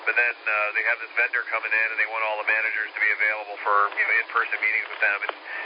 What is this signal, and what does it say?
Emergency Alert System (SAME) tones (header bursts), heard as harsh buzzing bursts, with speech. It is heard from 0.8 to 5.2 s.